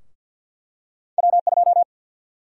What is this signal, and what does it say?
Morse code, heard as a keyed tone, about 52 wpm, 700 Hz. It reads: W2
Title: USA Call Sign Prefixes - 50wpm